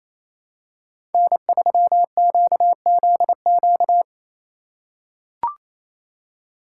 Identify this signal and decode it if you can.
Morse code, heard as a keyed tone, about 28 wpm, 700 Hz. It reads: N3QZQ
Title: USA Call Signs - 5 Characters - 2x Character Spacing - 28wpm